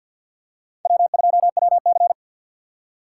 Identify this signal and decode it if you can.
Morse code, heard as a keyed tone, about 52 wpm, 700 Hz. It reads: W2WC
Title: USA Call Signs - 4 Characters - 2x Character Spacing - 50wpm